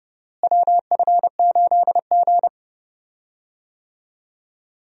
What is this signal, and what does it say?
Morse code, heard as a keyed tone, about 30 wpm, 700 Hz. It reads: WF8G
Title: USA Call Signs - 4 Characters - Repeated 3x - 30wpm